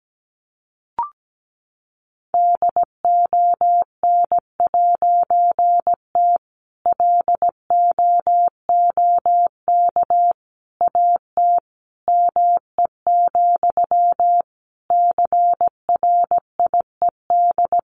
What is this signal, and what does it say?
Morse code, heard as a keyed tone, about 17 wpm, 700 Hz. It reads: DON'T LOOK AT ME, CRIED
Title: A Sound Of Thunder 17wpm